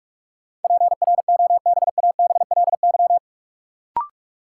Morse code, heard as a keyed tone, about 44 wpm, 700 Hz. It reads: PROBABLY
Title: Sentences from Top 700 Words - New Word Drill - 45wpm